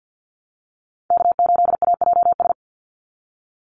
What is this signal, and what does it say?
Morse code, heard as a keyed tone, about 49 wpm, 700 Hz. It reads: K8AJS
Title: Call Signs - 50wpm